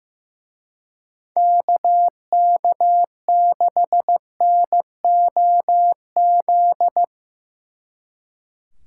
Morse code, heard as a keyed tone, about 17 wpm, 700 Hz. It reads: KK6NOZ
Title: Speed Racing - USA 2x3 Call Signs 15wpm